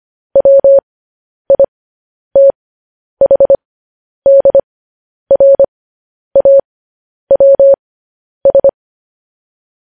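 Morse code, heard as a keyed tone, about 25 wpm, spaced as for 10 wpm, 550 Hz. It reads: WITHDRAWS
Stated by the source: Morse code news headlines practice transmission.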